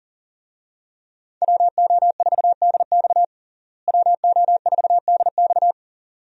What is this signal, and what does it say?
Morse code, heard as a keyed tone, about 40 wpm, 700 Hz. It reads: WO4DX WO4DX
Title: USA Call Signs - 5 Characters - Repeated 2x - 40wpm